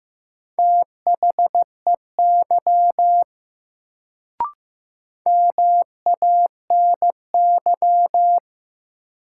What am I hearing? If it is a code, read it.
Morse code, heard as a keyed tone, about 15 wpm, 700 Hz. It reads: THEY MANY
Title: Sentences from Top 100 Words - New Word Drill - 15wpm